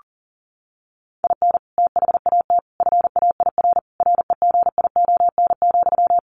Morse code, heard as a keyed tone, about 40 wpm, 700 Hz. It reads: IN THAT FAIR REGION,
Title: Sentences from Top 1500 Words 40wpm